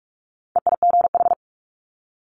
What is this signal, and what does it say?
Morse code, heard as a keyed tone, about 46 wpm, 700 Hz. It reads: EIGH
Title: Multi-letter Phonemes 45wpm